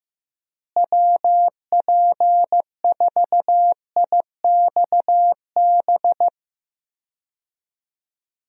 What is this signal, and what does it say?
Morse code, heard as a keyed tone, about 16 wpm, 700 Hz. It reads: WP4IXB
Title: USA Call Signs - 6 Characters - Repeated 2x - 15wpm